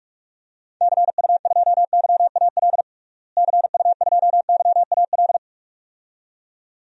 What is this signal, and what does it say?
Morse code, heard as a keyed tone, about 41 wpm, 700 Hz. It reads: CU2YAL CU2YAL
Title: DXCC-100 Call Signs - 6 Characters - Repeated 2x - 45wpm